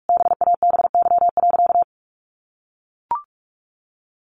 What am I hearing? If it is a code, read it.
Morse code, heard as a keyed tone, about 44 wpm, 700 Hz. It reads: BABY.
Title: Sentences from Top 1400 Words 45wpm